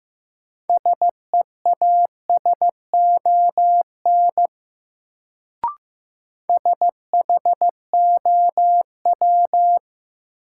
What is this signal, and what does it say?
Morse code, heard as a keyed tone, about 15 wpm, 700 Hz. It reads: SEASON SHOW
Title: Sets of 5 Words - New Word Drill - 15wpm